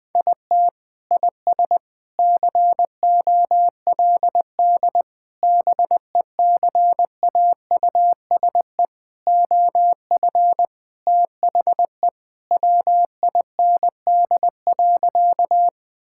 Morse code, heard as a keyed tone, about 21 wpm, 700 Hz. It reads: IT IS COLD BECAUSE OF THE WIND.
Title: Sentences with 3-Word Phrases - 20wpm